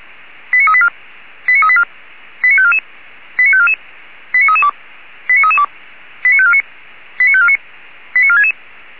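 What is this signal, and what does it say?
Five-tone selective call (ZVEI), heard as quick runs of five tones, tones of about 70 ms. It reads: 89384 89384 89544 89544 89332 89332 89549 89549 89466
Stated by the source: Received on an SDR; ZVEI five-tone call 89386.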